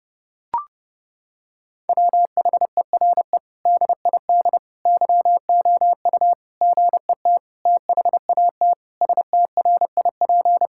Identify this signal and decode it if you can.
Morse code, heard as a keyed tone, about 30 wpm, 700 Hz. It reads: WHERE DID YOU GET THAT STRIP
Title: Sentences from Top 1600 Words 30wpm